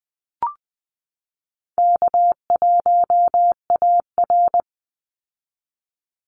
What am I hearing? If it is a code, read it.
Morse code, heard as a keyed tone, about 20 wpm, 700 Hz. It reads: K1AR
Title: Easy Call Signs Repeated 3x - 20wpm